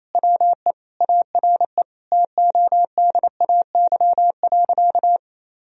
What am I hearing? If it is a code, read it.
Morse code, heard as a keyed tone, about 28 wpm, 700 Hz. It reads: WE ARE TODAY.
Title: Sentences from Top 1000 Words - Review Part 4 - 28wpm